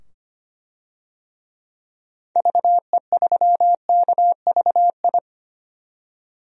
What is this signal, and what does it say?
Morse code, heard as a keyed tone, about 25 wpm, 700 Hz. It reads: VE3KVI
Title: DXCC-100 Call Signs - 6 Characters - 25wpm